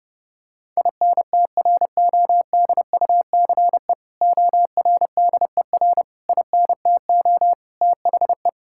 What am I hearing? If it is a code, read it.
Morse code, heard as a keyed tone, about 30 wpm, 700 Hz. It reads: INTRODUCE ORDER INTO THE
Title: Sentences from Top 1600 Words 30wpm